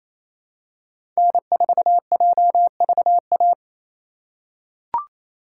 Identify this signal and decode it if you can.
Morse code, heard as a keyed tone, about 28 wpm, 700 Hz. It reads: N4JVA
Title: USA Call Signs - 5 Characters - 6x Character Spacing - 28wpm